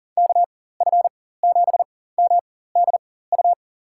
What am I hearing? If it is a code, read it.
Morse code, heard as a keyed tone, about 40 wpm, spaced as for 19 wpm, 700 Hz. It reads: KF7MDU
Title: USA Call Signs - 6 Characters - 4x Character Spacing - 40wpm